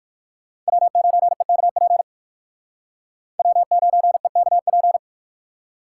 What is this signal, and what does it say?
Morse code, heard as a keyed tone, about 57 wpm, 700 Hz. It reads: W9EKP W9EKP